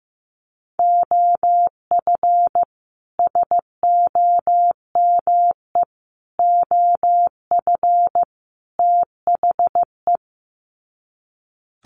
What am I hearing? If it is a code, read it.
Morse code, heard as a keyed tone, about 15 wpm, 700 Hz. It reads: OF SOME OF THE
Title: Sets of 4 Words 15wpm